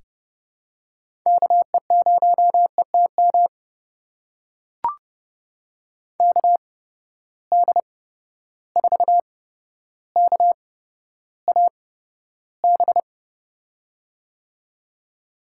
Morse code, heard as a keyed tone, about 30 wpm, spaced as for 13 wpm, 700 Hz. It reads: KE0ETM KD4KAB
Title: USA Call Signs - 6 Characters - 8x Character Spacing - 30wpm